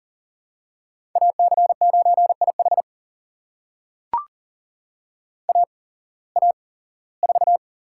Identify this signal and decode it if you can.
Morse code, heard as a keyed tone, about 40 wpm, spaced as for 21 wpm, 700 Hz. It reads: AC9IH AA4
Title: USA Call Signs - 5 Characters - 8x Character Spacing - 40wpm